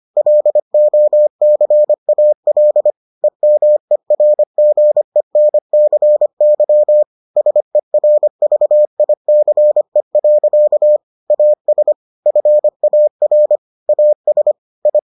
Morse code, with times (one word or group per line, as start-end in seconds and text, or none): 0.2-2.9 LOCAL
3.2-7.0 EMERGENCY
7.4-11.0 SERVICE.
11.3-11.9 AS
12.3-13.6 FAR
13.9-14.5 AS
14.9-15.0 I